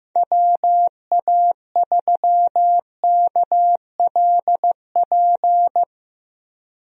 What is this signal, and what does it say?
Morse code, heard as a keyed tone, about 15 wpm, 700 Hz. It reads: WA3KLP
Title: USA Call Signs - 6 Characters - Repeated 2x - 15wpm